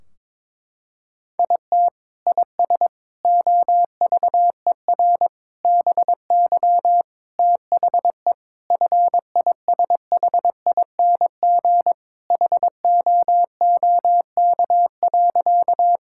Morse code, heard as a keyed tone, about 22 wpm, 700 Hz. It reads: IT IS OVER BY THE FISHING HOOK.
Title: Sentences from Top 1600 Words 22wpm